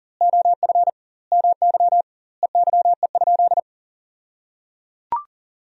Morse code, heard as a keyed tone, about 37 wpm, 700 Hz. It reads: OF MY EYE?
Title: 2x Word-Spacing - Sentences from Top 500 Words - 40wpm